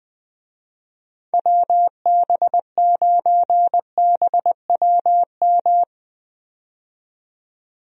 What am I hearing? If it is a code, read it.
Morse code, heard as a keyed tone, about 20 wpm, 700 Hz. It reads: WB9BWM
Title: USA Call Signs - 6 Characters - 20wpm